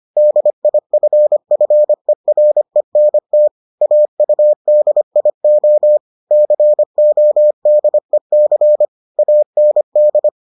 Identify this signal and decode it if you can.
Morse code, heard as a keyed tone, about 25 wpm, 600 Hz. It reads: DIFFERENT AUDIO CODEC AND